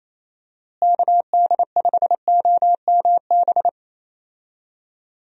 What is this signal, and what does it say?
Morse code, heard as a keyed tone, about 28 wpm, 700 Hz. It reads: KD5OMB
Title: USA Call Signs - 6 Characters - Repeated 3x - 28wpm